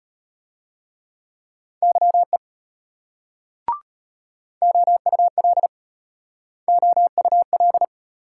Morse code, heard as a keyed tone, about 32 wpm, 700 Hz. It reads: YE OUL OUL